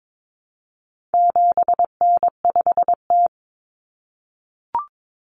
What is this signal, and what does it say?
Morse code, heard as a keyed tone, about 22 wpm, 700 Hz. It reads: 7N5T
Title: Common Cut Numbers - 4-Digit 22wpm